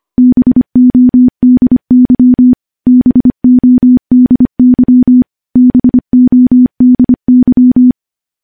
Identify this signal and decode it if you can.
Morse code, heard as a keyed tone, about 25 wpm, 250 Hz. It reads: BODY BODY BODY
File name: morse / morse-training-25WPM